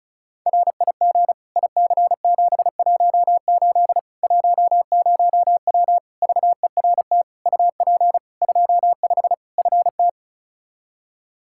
Morse code, heard as a keyed tone, about 36 wpm, 700 Hz. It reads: RIG IC718 10W VERT UP 25 FT
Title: Practice QSO - Exchange 2 - Info - 35wpm